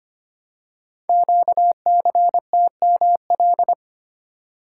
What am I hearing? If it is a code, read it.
Morse code, heard as a keyed tone, about 25 wpm, 700 Hz. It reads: QCTML
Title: Code Groups - 5 Letters - 25wpm